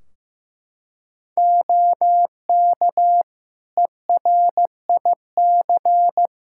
Morse code, heard as a keyed tone, about 15 wpm, 700 Hz. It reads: OK ERIC
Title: Practice QSO - Exchange 3 - More Info - 15wpm